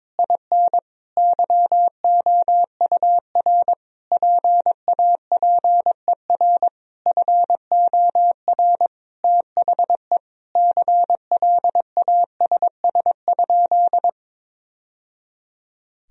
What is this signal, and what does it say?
Morse code, heard as a keyed tone, about 21 wpm, 700 Hz. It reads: IN YOUR PAPER FOR THE CLASS?
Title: Sentences from Top 1000 Words - Review Part 2 - 22wpm